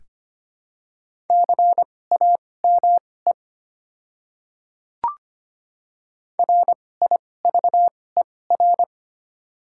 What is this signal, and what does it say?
Morse code, heard as a keyed tone, about 25 wpm, spaced as for 18 wpm, 690 Hz. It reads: CAME RIVER
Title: 2x Character Spacing - Top 300 Words - 25wpm